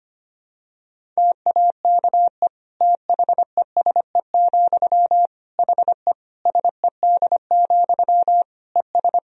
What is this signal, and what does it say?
Morse code, heard as a keyed tone, about 25 wpm, 700 Hz. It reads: TAKE THESE, HE SED, ES
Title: Sentences from Top 300 Words and Abbreviations 25wpm